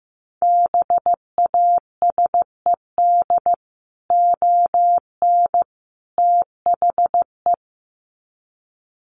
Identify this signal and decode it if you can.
Morse code, heard as a keyed tone, about 15 wpm, 700 Hz. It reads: BASED ON THE
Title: Sets of 3 Words 15wpm